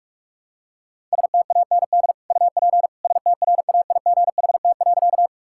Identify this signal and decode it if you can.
Morse code, heard as a keyed tone, about 43 wpm, 700 Hz. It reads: STAND UP STRAIGHT.
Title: Sentences from Top 1000 Words - Review Part 6 - 45wpm